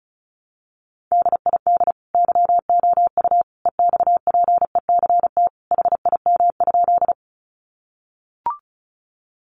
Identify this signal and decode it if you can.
Morse code, heard as a keyed tone, about 34 wpm, 700 Hz. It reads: DID YOU EXPECT HIM?
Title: Sentences from Top 1100 Words 35wpm